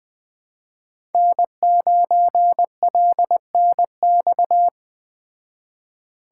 Morse code, heard as a keyed tone, about 20 wpm, 700 Hz. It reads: N9LNX